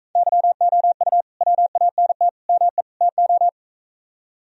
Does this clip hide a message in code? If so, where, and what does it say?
Yes: Morse code from 0.2 to 3.5 s, heard as a keyed tone, about 42 wpm, 700 Hz. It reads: YOU WANT ME TO